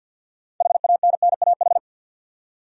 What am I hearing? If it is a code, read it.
Morse code, heard as a keyed tone, about 52 wpm, 700 Hz. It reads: HANNAH
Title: USA Common Names - 50wpm